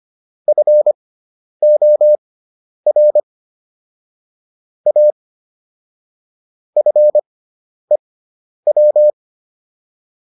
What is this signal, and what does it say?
Morse code, heard as a keyed tone, about 25 wpm, spaced as for 10 wpm, 600 Hz. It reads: FOR A FEW